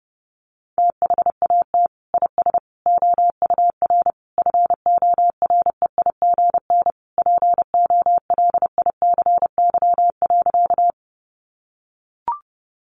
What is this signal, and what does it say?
Morse code, heard as a keyed tone, about 28 wpm, 700 Hz. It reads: THAT IS OUR FOREIGN POLICY.
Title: Sentences from Top 1400 Words 30wpm